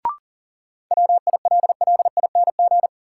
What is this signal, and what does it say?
Morse code, heard as a keyed tone, about 40 wpm, 700 Hz. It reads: WILLING
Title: Sentences from Top 1000 Words - New Word Drill - 40wpm